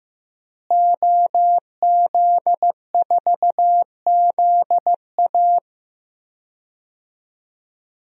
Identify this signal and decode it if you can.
Morse code, heard as a keyed tone, about 16 wpm, 700 Hz. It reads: OZ4ZA
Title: DXCC-100 Call Signs - 5 Characters - Repeated 2x - 15wpm